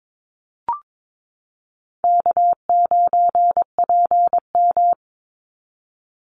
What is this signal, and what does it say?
Morse code, heard as a keyed tone, about 22 wpm, 700 Hz. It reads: K9PM
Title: NA SOTA Activator Callsigns 22wpm